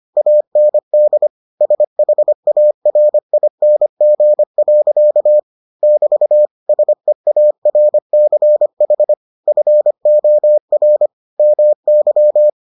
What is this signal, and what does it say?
Morse code, heard as a keyed tone, about 25 wpm, 600 Hz. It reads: AND SHARING. <BT> SEARCH FOR MY